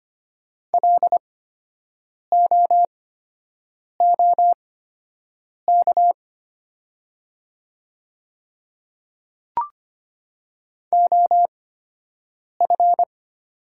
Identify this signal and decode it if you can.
Morse code, heard as a keyed tone, about 25 wpm, spaced as for 7 wpm, 700 Hz. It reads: LOOK OF
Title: 8x Character Spacing - Top 100 Words - 25wpm